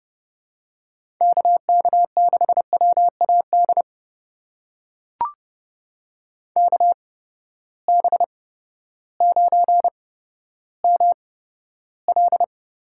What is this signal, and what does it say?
Morse code, heard as a keyed tone, about 30 wpm, spaced as for 14 wpm, 700 Hz. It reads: KK6WAD KB9ML
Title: USA Call Signs - 8x Character Spacing - 30wpm